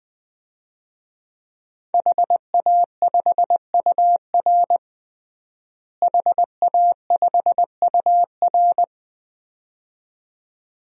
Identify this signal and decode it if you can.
Morse code, heard as a keyed tone, about 19 wpm, 700 Hz. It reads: HA5UR HA5UR
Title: DXCC-100 Call Signs - 5 Characters - Repeated 2x - 20wpm